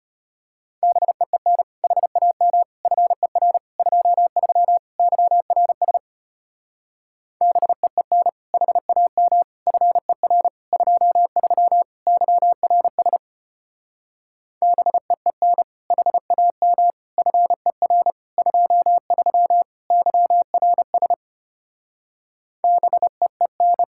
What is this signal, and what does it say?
Morse code, heard as a keyed tone, about 32 wpm, 700 Hz. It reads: BEEN HAM FER 23 YRS BEEN HAM FER 23 YRS BEEN HAM FER 23 YRS BEEN